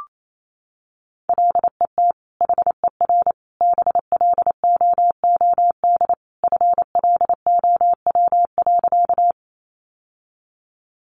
Morse code, heard as a keyed tone, about 28 wpm, 700 Hz. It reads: LET HER BLOOD FLOW.
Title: Sentences from Top 1200 Words 28wpm